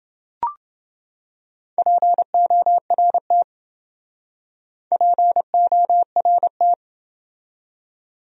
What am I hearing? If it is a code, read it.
Morse code, heard as a keyed tone, about 29 wpm, 700 Hz. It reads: PORT PORT